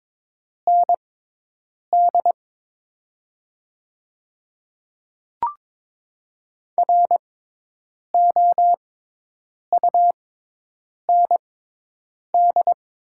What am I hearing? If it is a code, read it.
Morse code, heard as a keyed tone, about 22 wpm, spaced as for 8 wpm, 690 Hz. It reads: ND ROUND
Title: 6x Character Spacing - Top 500 Words - 22wpm